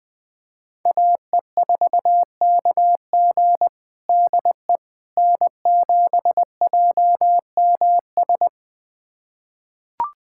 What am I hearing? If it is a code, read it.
Morse code, heard as a keyed tone, about 20 wpm, 700 Hz. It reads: AE4KG DE N7JMS
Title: USA Callsign DE Callsign - 5 Characters - 20wpm